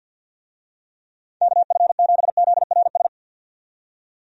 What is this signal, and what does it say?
Morse code, heard as a keyed tone, about 50 wpm, 700 Hz. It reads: KF7ZRS
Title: USA Call Signs - 6 Characters - 50wpm